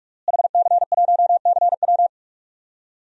Morse code, heard as a keyed tone, about 43 wpm, 700 Hz. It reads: HC1CW
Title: DXCC-100 Call Signs - 5 Characters - Repeated 3x - 45wpm